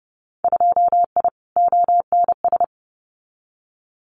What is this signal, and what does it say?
Morse code, heard as a keyed tone, about 30 wpm, 700 Hz. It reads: 2I ONS
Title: Winter Field Day Exchanges Encore 30wpm